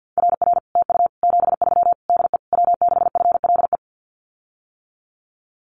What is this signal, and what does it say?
Morse code, heard as a keyed tone, about 50 wpm, 700 Hz. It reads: RR TU 73 DE W6FLE
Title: Common QSO Phrases Encore 50wpm